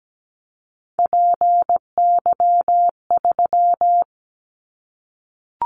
Morse code, heard as a keyed tone, about 17 wpm, 700 Hz. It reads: PY3
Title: Callsign Prefixes for Top 100 DXCC - 17wpm